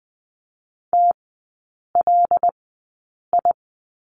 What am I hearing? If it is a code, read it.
Morse code, heard as a keyed tone, about 19 wpm, spaced as for 8 wpm, 700 Hz. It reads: TLI
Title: 121 Lesson 20e - Number 3 - Three Characters 20wpm